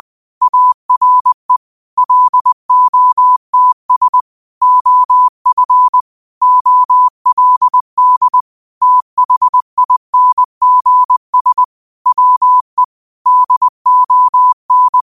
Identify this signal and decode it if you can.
Morse code, heard as a keyed tone, about 20 wpm, 1000 Hz. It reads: ARE LOTS OF OLD THINGS WE DON